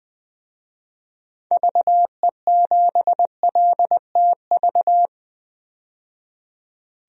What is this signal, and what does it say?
Morse code, heard as a keyed tone, about 20 wpm, 700 Hz. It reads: VE7LTV
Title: DXCC-100 Call Signs - 6 Characters - 20wpm